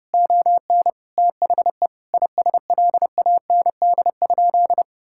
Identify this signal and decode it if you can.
Morse code, heard as a keyed tone, about 31 wpm, 700 Hz. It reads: ON THE ISLAND?
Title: Sentences from Top 1600 Words 30wpm